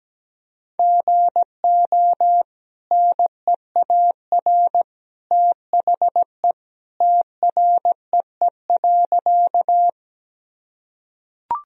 Morse code, heard as a keyed tone, about 17 wpm, 700 Hz. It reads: GO NEAR THE TREE.
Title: Sentences from Top 1000 Words - Review Part 1 - 17wpm